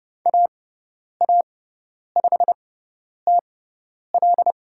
Morse code, heard as a keyed tone, about 30 wpm, spaced as for 10 wpm, 700 Hz. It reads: AA5TL
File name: Lesson 16g - Letter c - Call Signs 30wpm